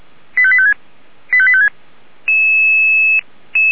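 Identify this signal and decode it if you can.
Five-tone selective call (ZVEI), heard as quick runs of five tones, tones of about 70 ms. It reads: 85856 85856
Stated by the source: ZVEI five-tone call 85854 received on SDR.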